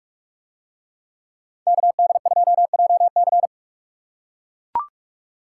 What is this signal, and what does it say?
Morse code, heard as a keyed tone, about 44 wpm, 700 Hz. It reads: KD2JC